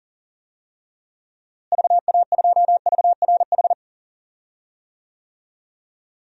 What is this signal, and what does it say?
Morse code, heard as a keyed tone, about 37 wpm, 700 Hz. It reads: VA2VRH
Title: DXCC-100 Call Signs - 6 Characters - Repeated 3x - 40wpm